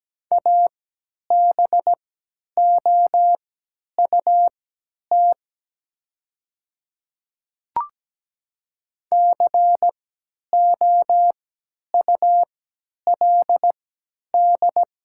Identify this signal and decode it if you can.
Morse code, heard as a keyed tone, about 17 wpm, spaced as for 10 wpm, 700 Hz. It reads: ABOUT COULD
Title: 3x Character Spacing - Top 100 Words - 17wpm